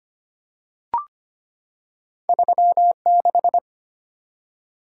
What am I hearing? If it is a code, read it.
Morse code, heard as a keyed tone, about 25 wpm, 700 Hz. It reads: 36
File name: Two Letter-Number Combinations - 25wpm